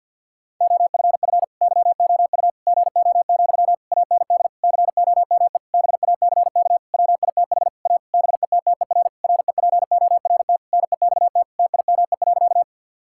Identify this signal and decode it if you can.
Morse code, heard as a keyed tone, about 51 wpm, 700 Hz. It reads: OFF YOU GO, AND COME BACK WITH A BETTER REPORT NEXT TIME.
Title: Sentences from Top 1000 Words - Review Part 6 - 50wpm